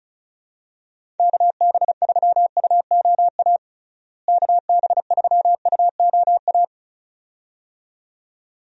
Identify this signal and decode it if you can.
Morse code, heard as a keyed tone, about 36 wpm, 700 Hz. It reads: KB3UOA KB3UOA
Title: USA Call Signs - 6 Characters - Repeated 2x - 35wpm